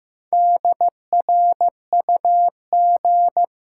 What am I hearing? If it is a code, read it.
Morse code, heard as a keyed tone, about 15 wpm, 700 Hz. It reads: DRUG
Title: Sets of 5 Words - New Word Drill - 15wpm